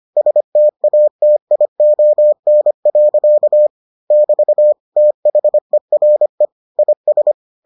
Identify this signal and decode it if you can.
Morse code, heard as a keyed tone, about 25 wpm, 600 Hz. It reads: STATION. <BT> THERE IS